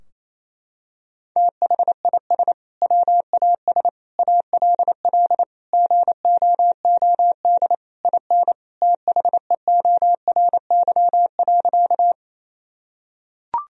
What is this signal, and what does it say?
Morse code, heard as a keyed tone, about 28 wpm, 700 Hz. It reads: THIS WAS ALL GOOD IN THEORY.
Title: Sentences from Top 1000 Words - Review Part 3 - 28wpm